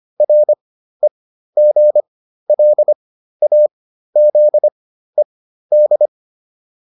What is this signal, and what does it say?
Morse code, heard as a keyed tone, about 25 wpm, spaced as for 13 wpm, 600 Hz. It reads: REGLAZED
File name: RE8 13